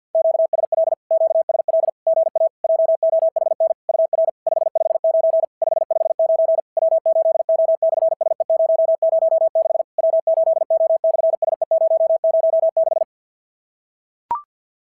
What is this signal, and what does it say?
Morse code, heard as a keyed tone, about 50 wpm, 650 Hz. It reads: QSL QSL GA JOHN UR 559 559 W8O/SE006 W8O/SE006
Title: SOTA QSO Practice 50wpm